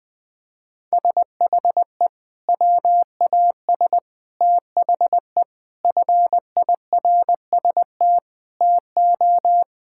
Morse code, heard as a keyed tone, about 20 wpm, 700 Hz. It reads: SHE WAS THE FIRST TO
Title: Sentences with 3-Word Phrases - 20wpm